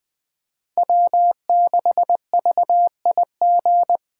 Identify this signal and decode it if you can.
Morse code, heard as a keyed tone, about 20 wpm, 700 Hz. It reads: W6VIG
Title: USA Call Signs - 5 Characters - 4x Character Spacing - 20wpm